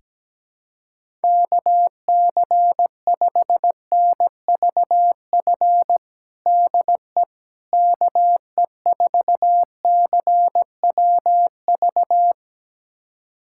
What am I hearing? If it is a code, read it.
Morse code, heard as a keyed tone, about 17 wpm, 700 Hz. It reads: KC5NVF DE KE4CWV
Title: USA Callsign DE Callsign 17wpm